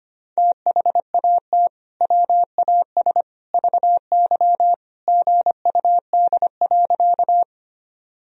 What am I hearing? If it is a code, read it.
Morse code, heard as a keyed tone, about 25 wpm, 700 Hz. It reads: THAT WAS VY GUD.
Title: Sentences from Top 200 Words and Abbreviations 25wpm